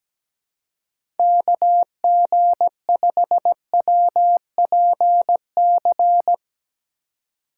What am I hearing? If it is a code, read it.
Morse code, heard as a keyed tone, about 17 wpm, 700 Hz. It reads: KG5WPC
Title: USA Call Signs - 6 Characters - 3x Character Spacing - 17wpm